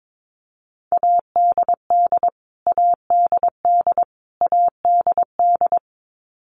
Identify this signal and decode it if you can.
Morse code, heard as a keyed tone, about 22 wpm, 700 Hz. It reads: ADD ADD ADD
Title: Top 300 Words Repeated 22wpm